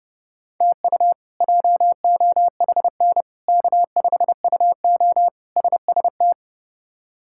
Morse code, heard as a keyed tone, about 30 wpm, 700 Hz. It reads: TU JOHN K5UO SST
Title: K1USN SST QSO Practice - 30wpm